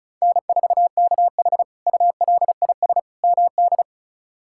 Morse code, heard as a keyed tone, about 34 wpm, 700 Hz. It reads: N4KH ULIS MD
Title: K1USN SST QSO Practice - 35wpm